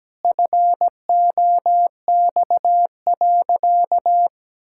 Morse code, heard as a keyed tone, about 17 wpm, 700 Hz. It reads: FOX.
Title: Pangram Challenge v2 - 17wpm